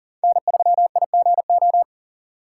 Morse code, heard as a keyed tone, about 51 wpm, 700 Hz. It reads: N3IGO